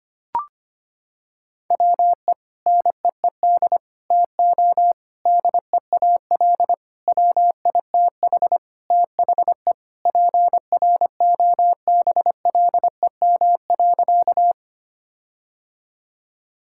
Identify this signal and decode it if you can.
Morse code, heard as a keyed tone, about 25 wpm, 700 Hz. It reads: WE NEED TO DEAL WITH THE PROBLEM.
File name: practice / 25wpm / Sentences with 4-Word Phrases - 25wpm